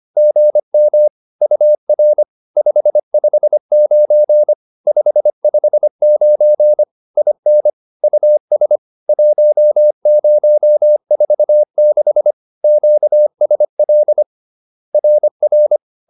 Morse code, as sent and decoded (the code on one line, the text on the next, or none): --. -- / ..- .-. / ..... ..... ----. / ..... ..... ----. / .. -. / ..- ... / .---- ----- ....- -.... / --.- ... .-.. / .-. .-.
GM UR 559 559 IN US 1046 QSL RR